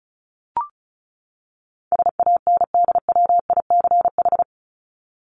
Morse code, heard as a keyed tone, about 34 wpm, 700 Hz. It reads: SANDWICH
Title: Top 1751-2000 Words 35wpm